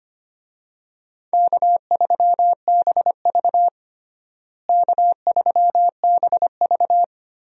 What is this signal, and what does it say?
Morse code, heard as a keyed tone, about 25 wpm, 700 Hz. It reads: K3BV K3BV
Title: USA Call Signs - 4 Characters - Repeated 3x - 25wpm